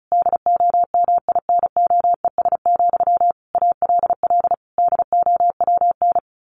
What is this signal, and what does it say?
Morse code, heard as a keyed tone, about 34 wpm, 700 Hz. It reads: DOMINOES, ALL DOWN
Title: A Sound Of Thunder 35wpm